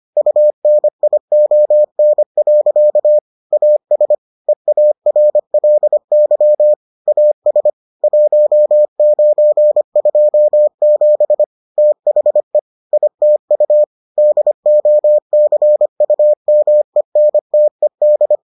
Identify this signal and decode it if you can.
Morse code, heard as a keyed tone, about 25 wpm, 600 Hz. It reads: UNION. AS EARLY AS 1927 THE ITU DOCUMENTED